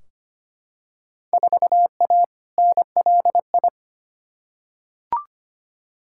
Morse code, heard as a keyed tone, about 25 wpm, 700 Hz. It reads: 4A NLI
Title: Field Day Exchanges Encore - 25wpm